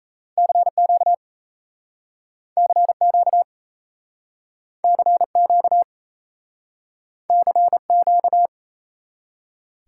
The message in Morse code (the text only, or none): CQ CQ CQ CQ